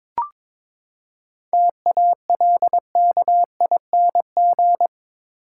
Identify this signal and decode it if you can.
Morse code, heard as a keyed tone, about 22 wpm, 700 Hz. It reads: TALKING
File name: Build A Phrase - 2-Word Phrases - 22wpm